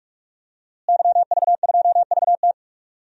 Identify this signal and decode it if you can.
Morse code, heard as a keyed tone, about 46 wpm, 700 Hz. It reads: YV2VT
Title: DXCC-100 Call Signs - 5 Characters - Repeated 3x - 45wpm